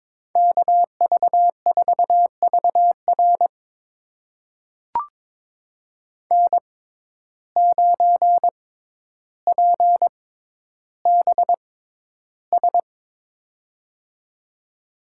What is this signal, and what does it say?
Morse code, heard as a keyed tone, about 22 wpm, spaced as for 11 wpm, 700 Hz. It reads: KV4VR N9PBS
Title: USA Call Signs - 5 Characters - 6x Character Spacing - 22wpm